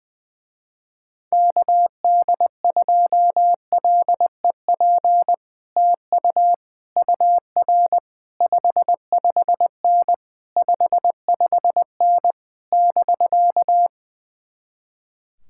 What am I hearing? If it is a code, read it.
Morse code, heard as a keyed tone, about 20 wpm, 700 Hz. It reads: KD2LEP TU UR 55N 55N <BK>
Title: Speed-Racing Common QSO Phrases Encore - 20wpm